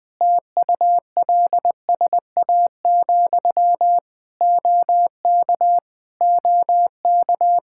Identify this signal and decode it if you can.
Morse code, heard as a keyed tone, about 20 wpm, 700 Hz. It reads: TULSA, OK OK
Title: Practice QSO - Exchange 1 - Beginning - 20wpm